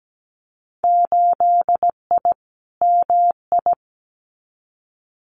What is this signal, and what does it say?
Morse code, heard as a keyed tone, about 17 wpm, 700 Hz. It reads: 8I MI
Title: Winter Field Day Exchanges Encore 17wpm